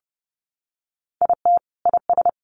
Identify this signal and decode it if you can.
Morse code, heard as a keyed tone, about 30 wpm, 700 Hz. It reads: IT IS
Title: Sentences from Top 500 Words 30wpm